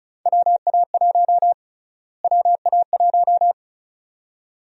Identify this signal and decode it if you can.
Morse code, heard as a keyed tone, about 36 wpm, 700 Hz. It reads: WA1 WA1
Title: USA Call Sign Prefixes - Repeated 2x - 35wpm